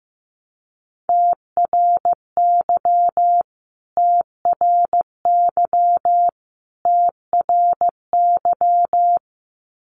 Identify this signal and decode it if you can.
Morse code, heard as a keyed tone, about 15 wpm, 700 Hz. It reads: TRY TRY TRY
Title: Top 500 Words Repeated 15wpm